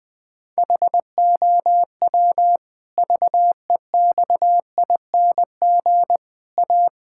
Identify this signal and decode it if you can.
Morse code, heard as a keyed tone, about 20 wpm, 700 Hz. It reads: HOW VEXING A
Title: Pangram Challenge v2 - 20wpm